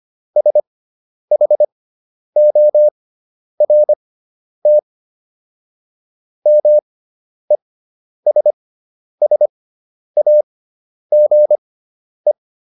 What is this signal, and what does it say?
Morse code, heard as a keyed tone, about 25 wpm, spaced as for 10 wpm, 600 Hz. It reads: SHORT MESSAGE